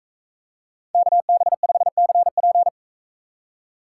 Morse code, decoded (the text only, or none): KB5CP